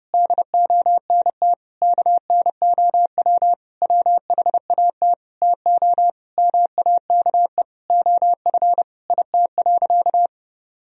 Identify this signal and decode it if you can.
Morse code, heard as a keyed tone, about 32 wpm, 700 Hz. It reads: DONT KNOW WHAT TO MAKE OF IT.
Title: Sentences from Top 1000 Words - Review Part 1 - 30wpm